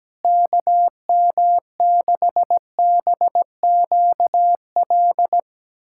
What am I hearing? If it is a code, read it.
Morse code, heard as a keyed tone, about 17 wpm, 700 Hz. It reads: KM6BQL
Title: USA Callsign DE Callsign - 6 Characters - 17wpm